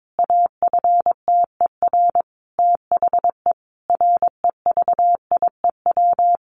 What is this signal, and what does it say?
Morse code, heard as a keyed tone, about 22 wpm, 700 Hz. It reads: AFTER THE REVIEW
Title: Sentences from Top 900 Words 22wpm